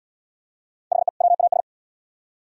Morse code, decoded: HE WAS

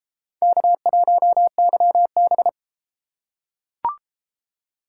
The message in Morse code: K1YB

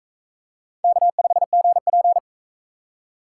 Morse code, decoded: K5GP

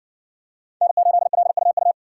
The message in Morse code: N8LUU

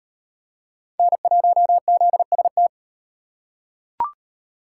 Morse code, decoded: N1ZST

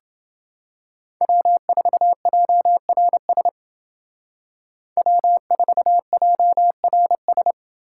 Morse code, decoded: W4JRS W4JRS

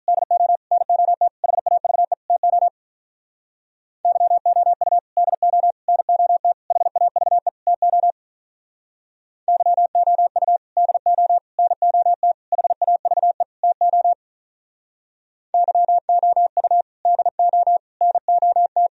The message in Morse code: DO NOT HAVE TO YOU DO NOT HAVE TO YOU DO NOT HAVE TO YOU DO NOT

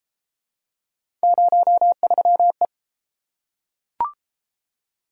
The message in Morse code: 03E

